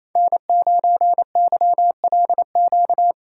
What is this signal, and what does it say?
Morse code, heard as a keyed tone, about 28 wpm, 700 Hz. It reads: N9YLQ
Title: USA Call Signs - 3x Character Spacing - 28wpm